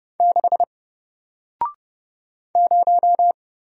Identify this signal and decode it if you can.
Morse code, heard as a keyed tone, about 31 wpm, 700 Hz. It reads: 6 0
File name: Single-Letter-Number-30wpm